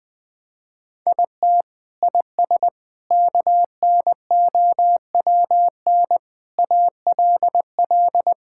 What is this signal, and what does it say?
Morse code, heard as a keyed tone, about 20 wpm, 700 Hz. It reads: IT IS KNOWN ALL
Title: Sentences with 4-Word Phrases - 20wpm